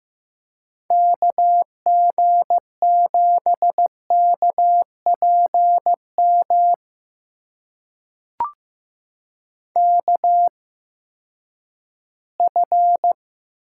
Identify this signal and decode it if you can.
Morse code, heard as a keyed tone, about 15 wpm, spaced as for 11 wpm, 700 Hz. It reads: KG7KPM KF
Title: USA Call Signs - 6 Characters - 8x Character Spacing - 15wpm